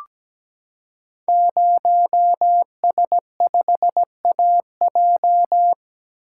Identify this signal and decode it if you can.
Morse code, heard as a keyed tone, about 17 wpm, 700 Hz. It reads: 0S5AJ
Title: Code Groups - 5 Letter:Numbers - 17wpm